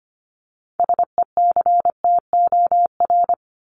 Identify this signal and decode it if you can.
Morse code, heard as a keyed tone, about 25 wpm, 700 Hz. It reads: SECTOR
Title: Top 1751-2000 Words 25wpm